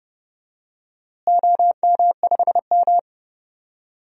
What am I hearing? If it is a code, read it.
Morse code, heard as a keyed tone, about 30 wpm, 700 Hz. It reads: OM5M
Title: DXCC-100 Call Signs - 4 Characters - Repeated 2x - 30wpm